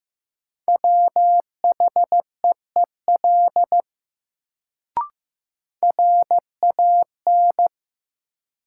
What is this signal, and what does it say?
Morse code, heard as a keyed tone, about 15 wpm, 700 Hz. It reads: WHEEL RAN
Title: Sentences from Top 500 Words - New Word Drill - 15wpm